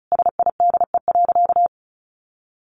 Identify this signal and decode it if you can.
Morse code, heard as a keyed tone, about 34 wpm, 700 Hz. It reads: SIDE.
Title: Sentences from Top 900 Words 35wpm